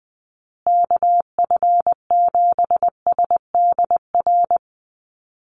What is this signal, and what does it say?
Morse code, heard as a keyed tone, about 20 wpm, 700 Hz. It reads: KF7SDR
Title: USA Call Signs Repeated 2x 20wpm